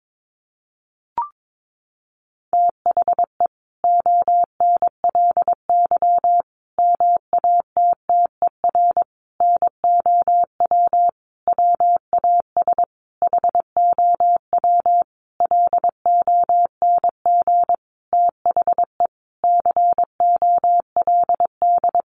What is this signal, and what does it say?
Morse code, heard as a keyed tone, about 22 wpm, 700 Hz. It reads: THE ONLY MATTER NOW WAS HOW LONG THE COLD
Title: Sentences from Top 1400 Words 22wpm